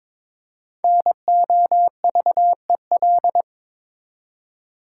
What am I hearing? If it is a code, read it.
Morse code, heard as a keyed tone, about 22 wpm, 700 Hz. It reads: NOVEL
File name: 5 Letter Words - First 3 Letters Given - 22wpm